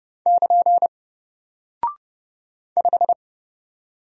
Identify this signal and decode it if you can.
Morse code, heard as a keyed tone, about 30 wpm, 700 Hz. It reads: <KN> 5
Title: Single Character (Letter:Number:Punctuation:Prosign) 30wpm